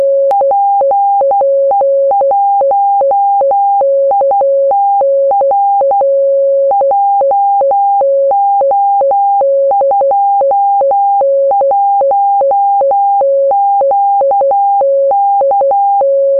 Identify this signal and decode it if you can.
Morse code, heard as a keyed tone, about 12 wpm, 800 Hz, with noise. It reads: PE1ITR JO21QK
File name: 432mhz / simple-pi4-beacon-in-gnuradio / result